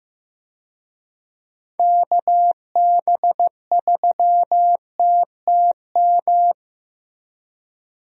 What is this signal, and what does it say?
Morse code, heard as a keyed tone, about 15 wpm, 700 Hz. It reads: KB3TTM
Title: USA Call Signs - 6 Characters - Repeated 3x - 15wpm